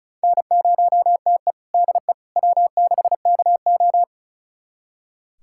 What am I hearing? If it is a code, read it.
Morse code, heard as a keyed tone, about 36 wpm, 700 Hz. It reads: N0TE DE W6KO
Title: USA Callsign DE Callsign - 4 Characters - 35wpm